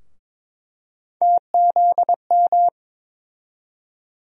Morse code, heard as a keyed tone, about 21 wpm, 700 Hz. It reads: TZM
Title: USA Call Sign Postfixes - 3 Letters 22wpm